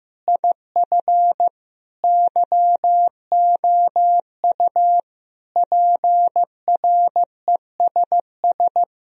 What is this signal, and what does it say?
Morse code, heard as a keyed tone, about 15 wpm, 700 Hz. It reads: IF YOU PRESS
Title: Sentences from Top 1000 Words - Review Part 1 - 15wpm